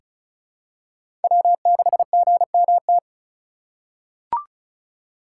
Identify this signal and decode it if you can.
Morse code, heard as a keyed tone, about 34 wpm, 700 Hz. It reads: W6GMT
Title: USA Call Signs - 5 Characters - 3x Character Spacing - 35wpm